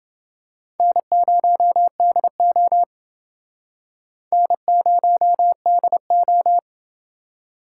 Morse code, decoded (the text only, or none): N0DO N0DO